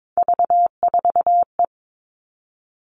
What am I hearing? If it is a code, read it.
Morse code, heard as a keyed tone, about 22 wpm, 700 Hz. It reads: V4E
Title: Three Challenging Letter-Number Combinations 22wpm